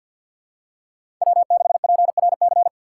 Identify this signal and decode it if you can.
Morse code, heard as a keyed tone, about 51 wpm, 700 Hz. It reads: W6PRC
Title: USA Call Signs - 5 Characters - 6x Character Spacing - 50wpm